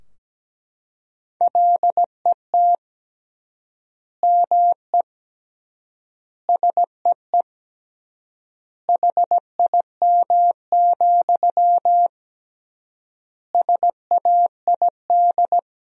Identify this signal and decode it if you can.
Morse code, heard as a keyed tone, about 17 wpm, 700 Hz. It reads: LET ME SEE HIM, SAID
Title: 3x Word-Spacing - Sentences from Top 400 Words - 17wpm